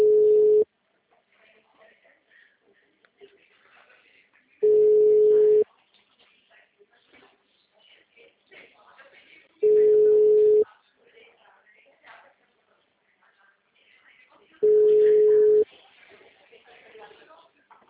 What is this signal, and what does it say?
Ringback tone (Europe, 425 Hz), with speech.